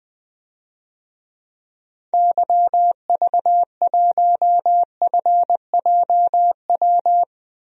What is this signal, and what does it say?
Morse code, heard as a keyed tone, about 20 wpm, 700 Hz. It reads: YV1FJW
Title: DXCC-100 Call Signs - 6 Characters - 20wpm